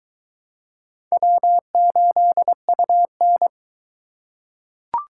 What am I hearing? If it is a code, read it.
Morse code, heard as a keyed tone, about 23 wpm, 700 Hz. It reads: W8UN